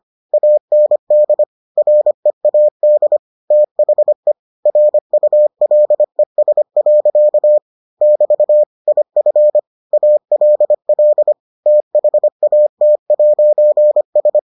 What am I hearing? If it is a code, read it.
Morse code, heard as a keyed tone, about 25 wpm, 600 Hz. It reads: AND READ THE RULES. <BT> IF ALL THAT'S